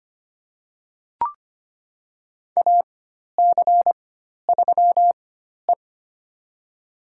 Morse code, heard as a keyed tone, about 25 wpm, spaced as for 12 wpm, 700 Hz. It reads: AC3E
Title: USA Call Signs - 4x Character Spacing - 25wpm